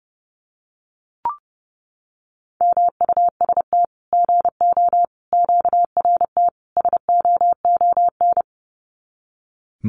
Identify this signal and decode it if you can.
Morse code, heard as a keyed tone, about 30 wpm, 700 Hz. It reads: MUST GO QRT SOON
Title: Common QSO Phrases 30wpm